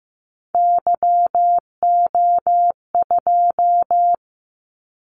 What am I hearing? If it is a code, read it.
Morse code, heard as a keyed tone, about 15 wpm, 700 Hz. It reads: YO2
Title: Callsign Prefixes for Top 25 DXCC - 15wpm